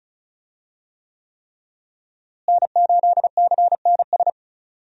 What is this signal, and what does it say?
Morse code, heard as a keyed tone, about 34 wpm, 700 Hz. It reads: N8CNS
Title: USA Call Signs - 5 Characters - Repeated 3x - 35wpm